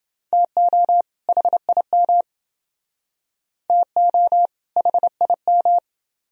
Morse code, heard as a keyed tone, about 27 wpm, 700 Hz. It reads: TO HIM TO HIM